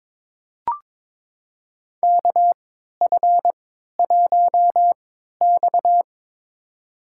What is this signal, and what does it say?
Morse code, heard as a keyed tone, about 22 wpm, spaced as for 12 wpm, 700 Hz. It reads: KF1X
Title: USA Call Signs - 4 Characters - 3x Character Spacing - 22wpm